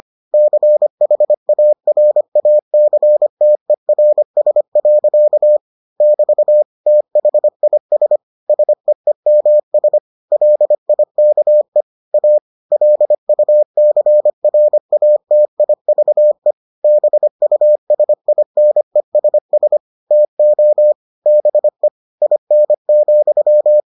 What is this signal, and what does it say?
Morse code, heard as a keyed tone, about 25 wpm, 600 Hz. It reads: CHARACTERS. <BT> THIS SEEMS LIKE A LUCRATIVE BUSINESS TO BE IN,